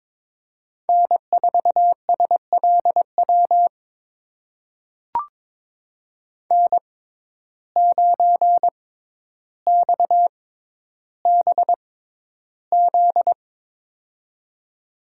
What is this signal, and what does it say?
Morse code, heard as a keyed tone, about 21 wpm, spaced as for 11 wpm, 700 Hz. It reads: N4SLW N9XBZ
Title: USA Call Signs - 6x Character Spacing - 22wpm